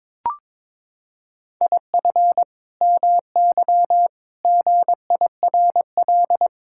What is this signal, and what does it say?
Morse code, heard as a keyed tone, about 22 wpm, 700 Hz. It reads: IF MY GIRL
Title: Sentences from Top 1600 Words 22wpm